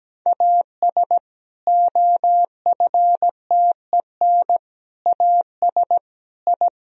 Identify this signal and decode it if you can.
Morse code, heard as a keyed tone, about 17 wpm, 700 Hz. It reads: AS OFTEN AS I
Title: Sentences from Top 1000 Words - Review Part 1 - 17wpm